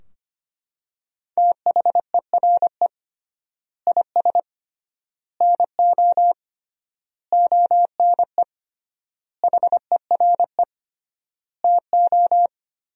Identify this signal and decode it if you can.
Morse code, heard as a keyed tone, about 25 wpm, 700 Hz. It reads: THERE IS NO ONE HERE TO
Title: 3x Word-Spacing - Sentences from Top 200 Words - 25wpm